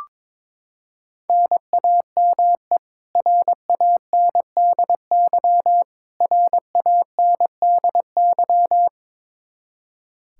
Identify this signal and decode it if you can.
Morse code, heard as a keyed tone, about 22 wpm, 700 Hz. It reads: NAME RANDY RANDY
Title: Practice QSO - Exchange 1 - Beginning - 22wpm